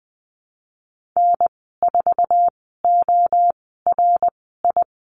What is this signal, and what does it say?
Morse code, heard as a keyed tone, about 19 wpm, spaced as for 14 wpm, 700 Hz. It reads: N4ORI
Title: 088 Lesson 15g - Number 5 - Call Signs 20wpm